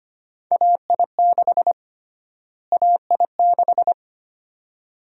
Morse code, heard as a keyed tone, about 25 wpm, 700 Hz. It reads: AI6 AI6
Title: USA Call Sign Prefixes - Repeated 2x - 25wpm